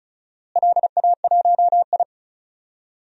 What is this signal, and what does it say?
Morse code, heard as a keyed tone, about 36 wpm, 700 Hz. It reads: LA1I
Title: DXCC-100 Call Signs - 4 Characters - Repeated 2x - 35wpm